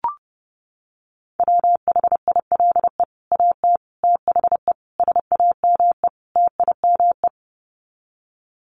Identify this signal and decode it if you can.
Morse code, heard as a keyed tone, about 30 wpm, 700 Hz. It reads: WHILE AT THE SAME TIME
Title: Sets of 5 Words 30wpm